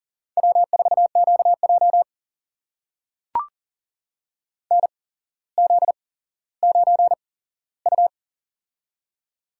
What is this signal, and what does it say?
Morse code, heard as a keyed tone, about 42 wpm, spaced as for 17 wpm, 700 Hz. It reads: W4QJ NZ9U
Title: USA Call Signs - 8x Character Spacing - 40wpm